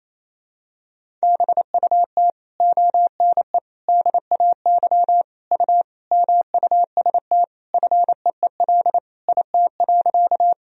Morse code, heard as a keyed tone, about 27 wpm, 700 Hz. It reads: BUT ONE DAY U MUST FEEL IT.
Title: Sentences from Top 400 Words and Abbreviations 28wpm